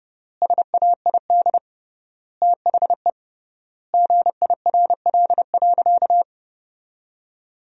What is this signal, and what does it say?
Morse code, heard as a keyed tone, about 30 wpm, 700 Hz. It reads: SAID THE GIRL.
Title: 3x Word-Spacing - Sentences from Top 400 Words - 30wpm